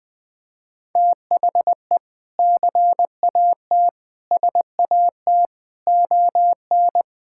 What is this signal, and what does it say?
Morse code, heard as a keyed tone, about 21 wpm, 700 Hz. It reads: THE CAT SAT ON
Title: Sentences with 3-Word Phrases - 20wpm